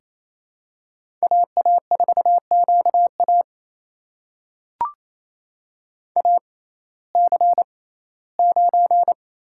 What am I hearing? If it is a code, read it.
Morse code, heard as a keyed tone, about 29 wpm, spaced as for 17 wpm, 700 Hz. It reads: AA4QA AC9
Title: USA Call Signs - 5 Characters - 6x Character Spacing - 28wpm